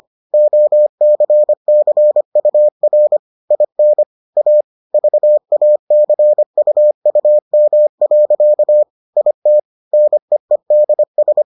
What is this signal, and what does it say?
Morse code, heard as a keyed tone, about 25 wpm, 600 Hz. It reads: OCCUR IN A VACUUM. IT NEEDS